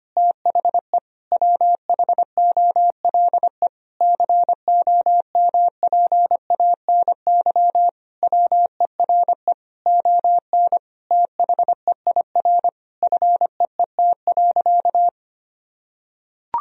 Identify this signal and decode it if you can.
Morse code, heard as a keyed tone, about 25 wpm, 700 Hz. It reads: THE WHOLE COMPANY WERE ON THEIR FEET.
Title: Sentences from Top 1000 Words - Review Part 3 - 25wpm